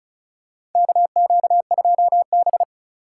Morse code, heard as a keyed tone, about 36 wpm, 700 Hz. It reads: KQ2B
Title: USA Call Signs - 4 Characters - 3x Character Spacing - 35wpm